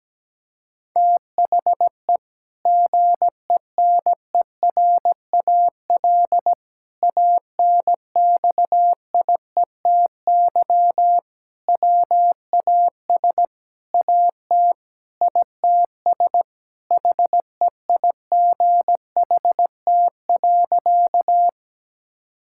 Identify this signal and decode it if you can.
Morse code, heard as a keyed tone, about 17 wpm, 700 Hz. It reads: THE GENERAL ANXIETY WAS AT ITS HEIGHT.
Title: Sentences from Top 1600 Words 17wpm